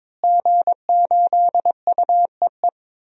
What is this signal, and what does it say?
Morse code, heard as a keyed tone, about 22 wpm, 700 Hz. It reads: G8UEE
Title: DXCC-100 Call Signs - 5 Characters - Repeated 3x - 22wpm